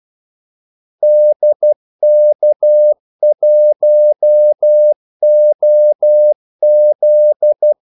Morse code, heard as a keyed tone, about 12 wpm, 600 Hz. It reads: DK1OZ
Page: 2 ekstra ordmellemrum mellem kaldesignalerne